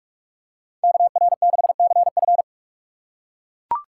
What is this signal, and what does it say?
Morse code, heard as a keyed tone, about 40 wpm, 700 Hz. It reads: KR6CF